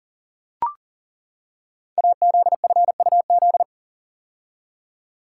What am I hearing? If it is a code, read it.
Morse code, heard as a keyed tone, about 41 wpm, 700 Hz. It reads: AZFUZ
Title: Code Groups - 5 Letters - 40wpm